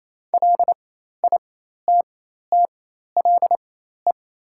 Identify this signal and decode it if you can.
Morse code, heard as a keyed tone, about 28 wpm, spaced as for 13 wpm, 700 Hz. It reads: LITTLE
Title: 4x Character Spacing - Top 300 Words - 28wpm